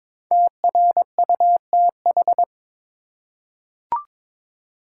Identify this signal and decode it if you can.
Morse code, heard as a keyed tone, about 22 wpm, 700 Hz. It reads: TRUTH
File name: 5 Letter Words - First 3 Letters Given - 22wpm